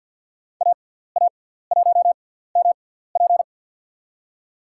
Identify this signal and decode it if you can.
Morse code, heard as a keyed tone, about 49 wpm, spaced as for 17 wpm, 700 Hz. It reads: AA1MP
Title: USA Call Signs - 5 Characters - 6x Character Spacing - 50wpm